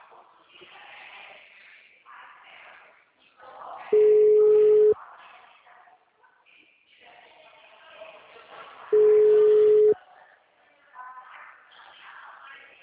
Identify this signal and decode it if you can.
Ringback tone (Europe, 425 Hz).